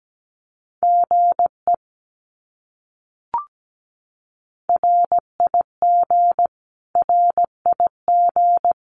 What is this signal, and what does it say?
Morse code, heard as a keyed tone, about 17 wpm, 700 Hz. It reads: GE RIG RIG
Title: Abbreviations Repeated 3x - 17wpm